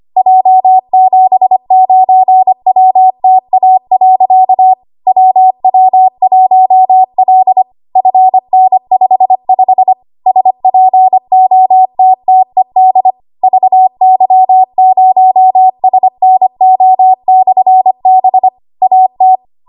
Morse code, heard as a keyed tone, about 25 wpm, 750 Hz. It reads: J79WTA. WW1L FN55 SPOTTED VY0SNO/B AT
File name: Morse / 25WPM